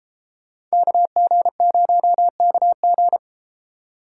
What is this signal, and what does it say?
Morse code, heard as a keyed tone, about 30 wpm, 700 Hz. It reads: KG0KG